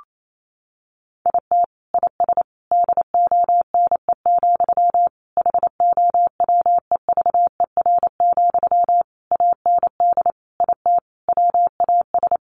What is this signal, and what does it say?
Morse code, heard as a keyed tone, about 28 wpm, 700 Hz. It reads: IT IS DONE, HOWEVER, AND IT WAS
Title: Sentences from Top 600 Words 28wpm